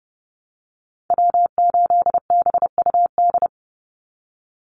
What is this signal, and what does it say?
Morse code, heard as a keyed tone, about 30 wpm, 700 Hz. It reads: W8BUD
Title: Call Signs Repeated 3x - 30wpm